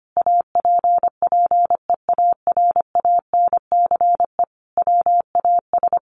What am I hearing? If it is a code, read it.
Morse code, heard as a keyed tone, about 25 wpm, 700 Hz. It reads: APPEARANCE WAS
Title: Sentences from Top 1100 Words 25wpm